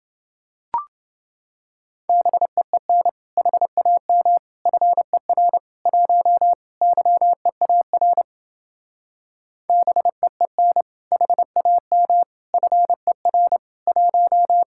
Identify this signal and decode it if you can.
Morse code, heard as a keyed tone, about 28 wpm, 700 Hz. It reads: BEEN HAM FER 1 YEAR BEEN HAM FER 1